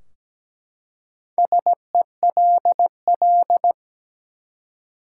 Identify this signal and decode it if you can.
Morse code, heard as a keyed tone, about 17 wpm, 700 Hz. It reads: SELL
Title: Four Letter Words - 17wpm